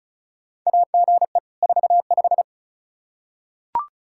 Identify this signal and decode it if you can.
Morse code, heard as a keyed tone, about 34 wpm, 700 Hz. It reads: AGE 45
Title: Practice QSO - Exchange 3 - More Info - 35wpm